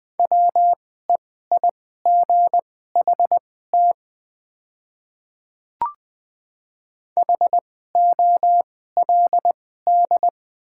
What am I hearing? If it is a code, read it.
Morse code, heard as a keyed tone, about 20 wpm, spaced as for 14 wpm, 700 Hz. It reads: WEIGHT HOLD
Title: 2x Character Spacing - Top 500 Words - 20wpm